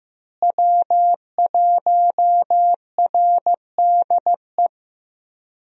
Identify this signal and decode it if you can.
Morse code, heard as a keyed tone, about 15 wpm, 700 Hz. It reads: W1RDE
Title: USA Call Signs - 5 Characters - Repeated 2x - 15wpm